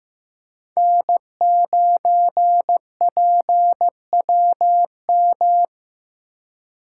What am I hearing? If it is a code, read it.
Morse code, heard as a keyed tone, about 15 wpm, 700 Hz. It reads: N9PWM
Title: USA Call Signs - 5 Characters - 4x Character Spacing - 15wpm